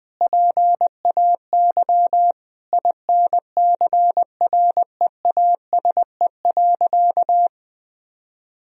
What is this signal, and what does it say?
Morse code, heard as a keyed tone, about 20 wpm, 700 Hz. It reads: PAY INCREASE.
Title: Sentences from Top 1000 Words - Review Part 4 - 20wpm